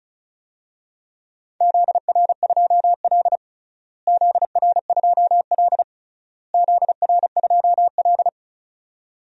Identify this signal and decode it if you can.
Morse code, heard as a keyed tone, about 36 wpm, 700 Hz. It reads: ZR2L ZR2L ZR2L
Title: DXCC-100 Call Signs - 4 Characters - Repeated 3x - 35wpm